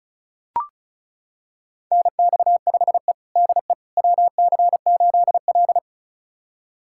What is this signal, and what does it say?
Morse code, heard as a keyed tone, about 36 wpm, 700 Hz. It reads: NX5E DE WC8L
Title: USA Callsign DE Callsign - 4 Characters - 35wpm